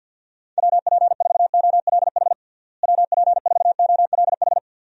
Morse code, heard as a keyed tone, about 52 wpm, 700 Hz. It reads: WP4OLH WP4OLH
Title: USA Call Signs - 6 Characters - Repeated 3x - 50wpm